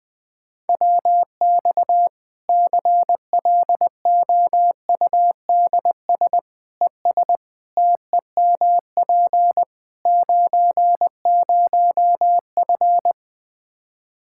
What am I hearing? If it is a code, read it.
Morse code, heard as a keyed tone, about 20 wpm, 700 Hz. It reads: WX CLOUDS ES TEMP 90F
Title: Practice QSO - Exchange 2 - Info - 20wpm